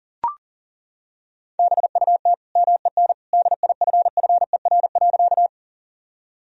Morse code, heard as a keyed tone, about 37 wpm, 700 Hz. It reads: BUT MEN DIFFER.
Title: Sentences from Top 1000 Words - Review Part 1 - 40wpm